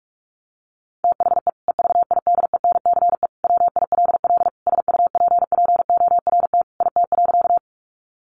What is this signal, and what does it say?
Morse code, heard as a keyed tone, about 44 wpm, 700 Hz. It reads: THE EVIDENCE WILL SUPPORT IT.
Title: Sentences from Top 1300 Words 45wpm